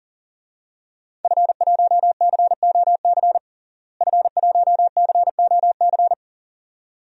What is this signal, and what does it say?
Morse code, heard as a keyed tone, about 37 wpm, 700 Hz. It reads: F1COC F1COC
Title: DXCC-100 Call Signs - 5 Characters - Repeated 2x - 40wpm